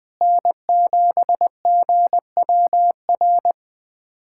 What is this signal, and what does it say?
Morse code, heard as a keyed tone, about 20 wpm, 700 Hz. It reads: N7GWR
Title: USA Callsign DE Callsign - 5 Characters - 20wpm